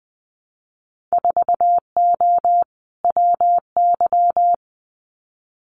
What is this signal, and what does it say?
Morse code, heard as a keyed tone, about 20 wpm, 700 Hz. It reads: 4O WY
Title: Winter Field Day Exchanges Encore 20wpm